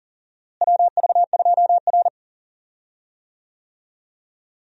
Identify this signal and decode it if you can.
Morse code, heard as a keyed tone, about 40 wpm, 700 Hz. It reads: WV2R
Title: USA Call Signs - 4 Characters - Repeated 2x - 40wpm